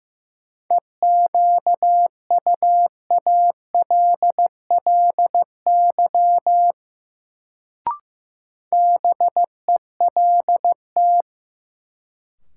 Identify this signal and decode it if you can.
Morse code, heard as a keyed tone, about 16 wpm, 700 Hz. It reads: EQUALLY BELT
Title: Sentences from Top 1400 Words - New Word Drill - 15wpm